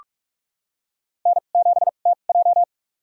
Morse code, heard as a keyed tone, about 44 wpm, spaced as for 32 wpm, 700 Hz. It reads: N7TJ
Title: USA Call Signs - 4 Characters - 2x Character Spacing - 45wpm